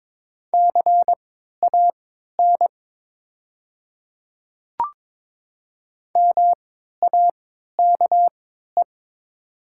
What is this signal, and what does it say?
Morse code, heard as a keyed tone, about 22 wpm, spaced as for 13 wpm, 700 Hz. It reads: CAN MAKE
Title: 3x Character Spacing - Top 100 Words - 22wpm